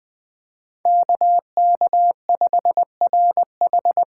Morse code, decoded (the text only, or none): KK5RH